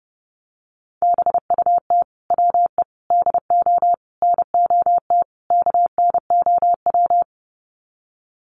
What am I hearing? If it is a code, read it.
Morse code, heard as a keyed tone, about 30 wpm, 700 Hz. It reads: BUT WE DO NOT KNOW
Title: Sets of 5 Words Encore2 30wpm